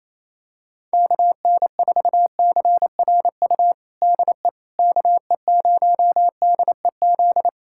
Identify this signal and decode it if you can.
Morse code, heard as a keyed tone, about 28 wpm, 700 Hz. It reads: KN4CRU DE KE0DEZ
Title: USA Callsign DE Callsign - 6 Characters - 28wpm